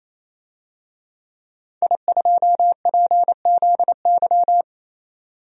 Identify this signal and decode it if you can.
Morse code, heard as a keyed tone, about 28 wpm, 700 Hz. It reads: I2PZY
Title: DXCC-100 Call Signs - 5 Characters - 28wpm